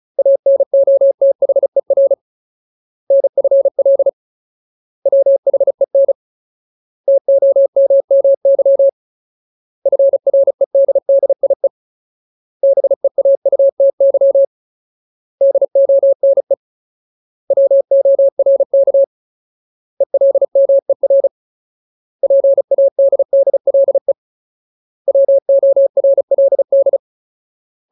Morse code, heard as a keyed tone, about 34 wpm, 550 Hz. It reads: ANOTHER NFL WHEN TOMMY FREDDIE BEAUTY DONE WORK ELMER PADDLE WORLD
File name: Ultimate Word List / 35WPM Ultimate